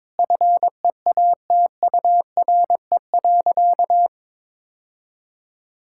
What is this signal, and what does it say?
Morse code, heard as a keyed tone, about 22 wpm, 700 Hz. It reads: FEATURE.
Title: Sentences from Top 1000 Words - Review Part 5 - 22wpm